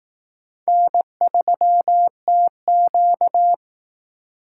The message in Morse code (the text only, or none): N3TQ